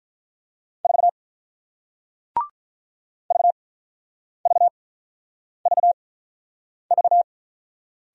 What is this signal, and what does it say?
Morse code, heard as a keyed tone, about 48 wpm, spaced as for 9 wpm, 690 Hz. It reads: 4 VVVV